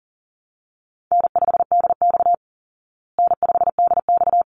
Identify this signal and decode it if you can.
Morse code, heard as a keyed tone, about 40 wpm, 700 Hz. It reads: N5DX N5DX
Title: Easy Call Signs Repeated 2x - 40wpm